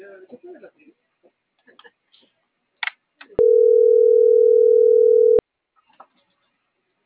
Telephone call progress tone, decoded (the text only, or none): ringback tone